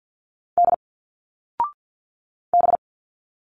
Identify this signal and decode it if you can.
Morse code, heard as a keyed tone, about 50 wpm, 700 Hz. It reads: D B
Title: Single Letters 50wpm